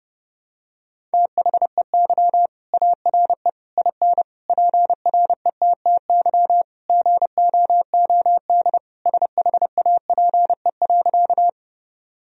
Morse code, heard as a keyed tone, about 30 wpm, 700 Hz. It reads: THEY ARE IN PRETTY GOOD SHAPE.
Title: Sentences from Top 1000 Words - Review Part 3 - 30wpm